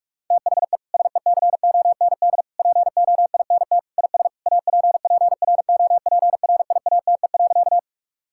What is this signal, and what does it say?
Morse code, heard as a keyed tone, about 41 wpm, 700 Hz. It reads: THE SECOND POINT IS APPROPRIATE.
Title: Sentences from Top 1000 Words - Review Part 5 - 45wpm